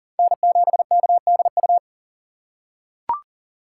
Morse code, heard as a keyed tone, about 40 wpm, 700 Hz. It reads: N7KDU
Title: USA Call Signs - 5 Characters - 3x Character Spacing - 40wpm